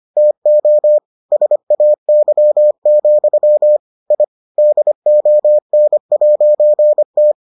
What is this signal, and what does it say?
Morse code, heard as a keyed tone, about 25 wpm, 600 Hz. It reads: TO SAY, I DON'T